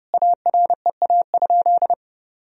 Morse code, heard as a keyed tone, about 30 wpm, 700 Hz. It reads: AREA?
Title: Sentences from Top 1000 Words - Review Part 2 - 30wpm